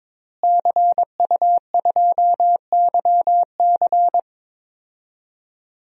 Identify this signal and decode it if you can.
Morse code, heard as a keyed tone, about 22 wpm, 700 Hz. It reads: CU2YC
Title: DXCC-100 Call Signs - 5 Characters - Repeated 2x - 22wpm